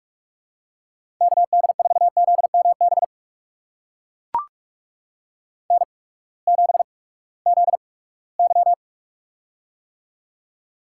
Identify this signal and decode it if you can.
Morse code, heard as a keyed tone, about 43 wpm, 690 Hz. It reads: K D 4 Z M B N 7 Z Y
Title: USA Call Signs - 8x Character Spacing - 45wpm